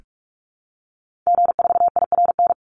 Morse code, heard as a keyed tone, about 44 wpm, 700 Hz. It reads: G4IRN
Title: Call Signs Repeated 3x - 45wpm